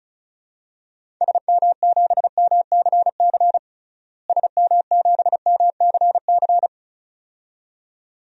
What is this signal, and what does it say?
Morse code, heard as a keyed tone, about 36 wpm, 700 Hz. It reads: SM7MCC SM7MCC
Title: DXCC-100 Call Signs - 6 Characters - Repeated 2x - 35wpm